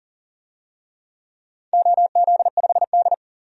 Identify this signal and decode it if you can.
Morse code, heard as a keyed tone, about 40 wpm, 700 Hz. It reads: OZ5D
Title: DXCC-100 Call Signs - 4 Characters - 40wpm